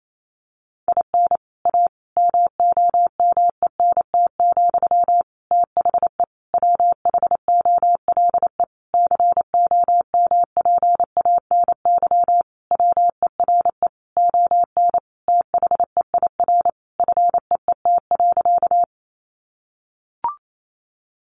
Morse code, heard as a keyed tone, about 28 wpm, 700 Hz. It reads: IN A MOMENT, THE WHOLE COMPANY WERE ON THEIR FEET.
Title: Sentences from Top 1000 Words 28wpm